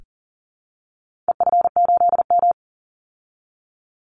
Morse code, heard as a keyed tone, about 40 wpm, 700 Hz. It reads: EF8M
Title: Call Signs Repeated 3x - 40wpm